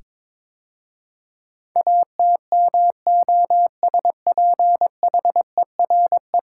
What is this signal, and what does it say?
Morse code, heard as a keyed tone, about 23 wpm, 700 Hz. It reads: ATMOSPHERE
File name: practice / 22wpm / 10 Letter Words - First 3 Letters Given - 22wpm